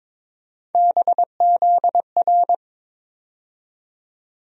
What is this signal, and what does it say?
Morse code, heard as a keyed tone, about 22 wpm, 700 Hz. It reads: BZR
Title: USA Call Sign Postfixes - 3 Letters 22wpm